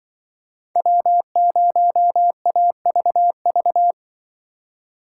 Morse code, heard as a keyed tone, about 24 wpm, 700 Hz. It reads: W0AVV
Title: Speed Racing - USA 1x3 Call Signs 20wpm